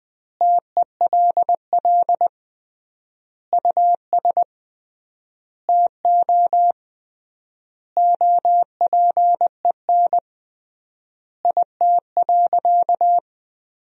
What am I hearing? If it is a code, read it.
Morse code, heard as a keyed tone, about 20 wpm, 700 Hz. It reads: TELL US TO OPEN IT.
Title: 3x Word-Spacing - Sentences from Top 300 Words - 20wpm